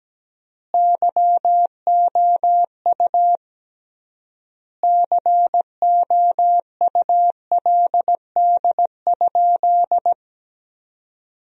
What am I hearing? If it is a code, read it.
Morse code, heard as a keyed tone, about 17 wpm, 700 Hz. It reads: YOU COULD?
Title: 3x Word-Spacing - Sentences from Top 400 Words - 17wpm